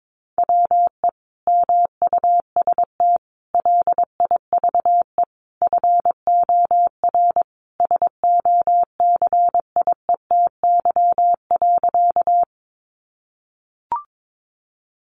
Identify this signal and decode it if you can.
Morse code, heard as a keyed tone, about 22 wpm, 700 Hz. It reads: WE MUST LIVE FOR SOCIETY.
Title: Sentences from Top 700 Words 22wpm